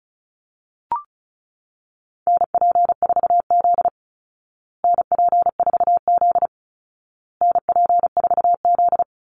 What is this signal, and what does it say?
Morse code, heard as a keyed tone, about 34 wpm, 700 Hz. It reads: NP4Z NP4Z NP4Z
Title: Call Signs Repeated 2x - 35wpm